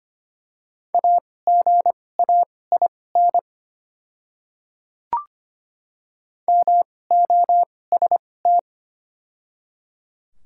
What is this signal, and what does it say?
Morse code, heard as a keyed tone, about 25 wpm, spaced as for 18 wpm, 700 Hz. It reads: AGAIN MOST
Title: 2x Character Spacing - Top 200 Words - 25wpm